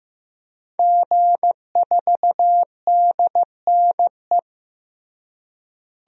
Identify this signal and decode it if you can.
Morse code, heard as a keyed tone, about 15 wpm, 700 Hz. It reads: G4DNE
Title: DXCC-100 Call Signs - 5 Characters - Repeated 3x - 15wpm